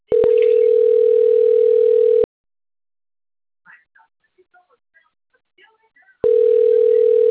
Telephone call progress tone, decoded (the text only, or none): ringback tone